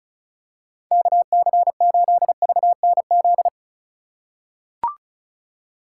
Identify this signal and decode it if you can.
Morse code, heard as a keyed tone, about 34 wpm, 700 Hz. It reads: KC8VNZ
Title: USA Call Signs - 6 Characters - 2x Character Spacing - 35wpm